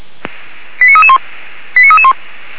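Five-tone selective call (ZVEI), heard as quick runs of five tones, tones of about 70 ms. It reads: 89331 89331
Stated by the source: ZVEI five-tone selective call 89320 received via SDR.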